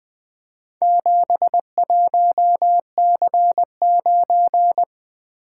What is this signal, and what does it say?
Morse code, heard as a keyed tone, about 23 wpm, 700 Hz. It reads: 71C9
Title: Speed-Racing - Four Letter-Number Combinations - 20wpm